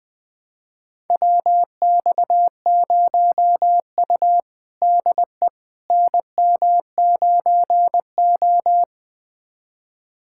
Morse code, heard as a keyed tone, about 21 wpm, 700 Hz. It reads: WX0U DE NM9O
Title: USA Callsign DE Callsign - 4 Characters - 20wpm